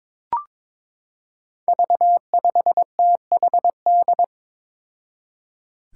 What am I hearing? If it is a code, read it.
Morse code, heard as a keyed tone, about 21 wpm, 700 Hz. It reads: V5THD
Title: Five Challenging Letter-Number Combinations 22wpm